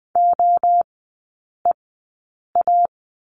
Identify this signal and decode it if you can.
Morse code, heard as a keyed tone, about 20 wpm, spaced as for 8 wpm, 700 Hz. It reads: OEA
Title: 100 Lesson 17e - Letter u - Three Characters 20wpm